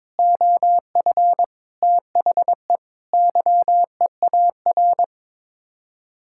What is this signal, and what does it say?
Morse code, heard as a keyed tone, about 21 wpm, 700 Hz. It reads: OF THE YEAR
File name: practice / 22wpm / Sets of 7 Words - 22wpm